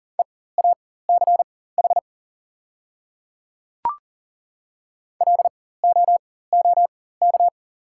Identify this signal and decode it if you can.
Morse code, heard as a keyed tone, about 40 wpm, spaced as for 19 wpm, 700 Hz. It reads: EACH LOOK
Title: 4x Character Spacing - Top 100 Words - 40wpm